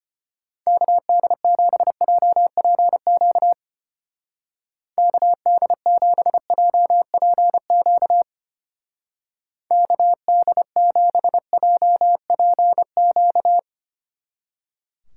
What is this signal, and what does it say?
Morse code, heard as a keyed tone, about 32 wpm, 700 Hz. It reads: KD7JPQ KD7JPQ KD7JPQ